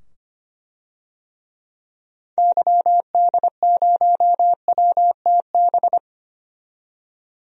Morse code, heard as a keyed tone, about 25 wpm, 700 Hz. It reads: YD0WTB
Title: DXCC-100 Call Signs - 6 Characters - 25wpm